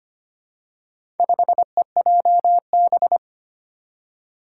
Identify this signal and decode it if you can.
Morse code, heard as a keyed tone, about 25 wpm, 700 Hz. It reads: 5EJB
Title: Four Challenging Letter-Number Combinations 25wpm